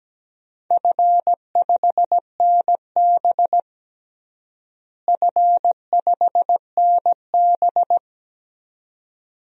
Morse code, heard as a keyed tone, about 17 wpm, 700 Hz. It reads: F5NB F5NB
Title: DXCC-100 Call Signs - 4 Characters - Repeated 2x - 17wpm